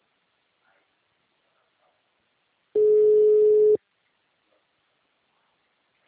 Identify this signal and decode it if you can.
Ringback tone (Europe, 425 Hz).